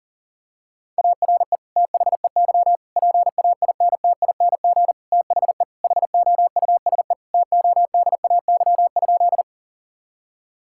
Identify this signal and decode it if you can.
Morse code, heard as a keyed tone, about 40 wpm, 700 Hz. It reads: ARE THEY PAINTING THE HOUSE TODAY?
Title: Sentences from Top 1000 Words - Review Part 6 - 40wpm